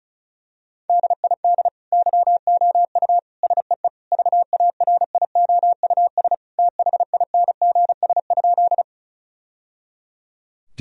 Morse code, heard as a keyed tone, about 36 wpm, 700 Hz. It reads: DID YOU SEE VARIOUS THINGS?
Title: Sentences from Top 1000 Words - Review Part 4 - 35wpm